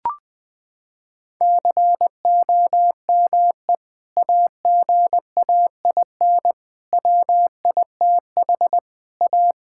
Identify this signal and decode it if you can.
Morse code, heard as a keyed tone, about 20 wpm, 700 Hz. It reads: COME AGAIN WITH A
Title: Sentences from Top 1000 Words - Review Part 6 - 20wpm